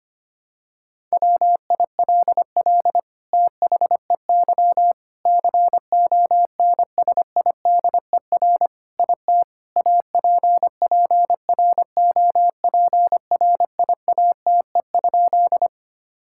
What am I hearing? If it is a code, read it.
Morse code, heard as a keyed tone, about 25 wpm, 700 Hz. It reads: WILL THEY CONSIDER IT APPROPRIATE?
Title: Sentences from Top 1000 Words - Review Part 5 - 25wpm